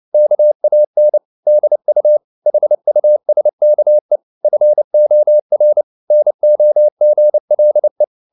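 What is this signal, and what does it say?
Morse code, heard as a keyed tone, about 29 wpm, 600 Hz. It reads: KAN DU HUSKE FOR NOGLE
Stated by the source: Fast (QRQ) Morse at 29 wpm, from CW med Gnister.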